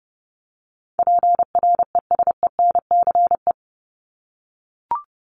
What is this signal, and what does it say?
Morse code, heard as a keyed tone, about 30 wpm, 700 Hz. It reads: PRESENCE
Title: Top 1751-2000 Words 30wpm